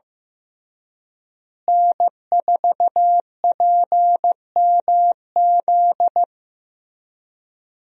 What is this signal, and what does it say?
Morse code, heard as a keyed tone, about 16 wpm, 700 Hz. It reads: N4PMZ
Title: USA Call Signs - 5 Characters - Repeated 3x - 15wpm